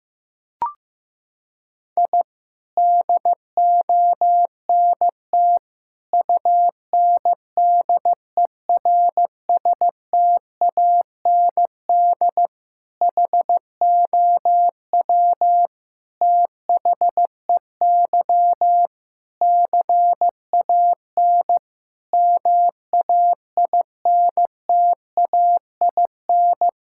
Morse code, heard as a keyed tone, about 15 wpm, 700 Hz. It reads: I DONT UNDERSTAND HOW THEY CAN MAINTAIN
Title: Sentences from Top 1000 Words - Review Part 5 - 15wpm